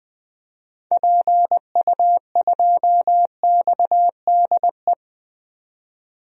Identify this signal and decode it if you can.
Morse code, heard as a keyed tone, about 20 wpm, 700 Hz. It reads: PU2XDE
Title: DXCC-100 Call Signs - 6 Characters - 20wpm